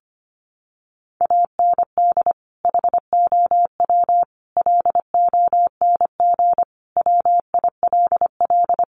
Morse code, heard as a keyed tone, about 25 wpm, 700 Hz. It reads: AND HOW LONG WILL
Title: Sentences from Top 1200 Words 25wpm